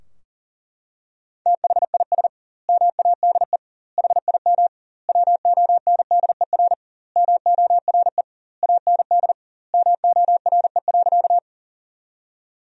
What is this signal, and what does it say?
Morse code, heard as a keyed tone, about 38 wpm, 700 Hz. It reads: THIS MADE HIM WONDER MORE AND MORE.
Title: 2x Word-Spacing - Sentences from Top 500 Words - 40wpm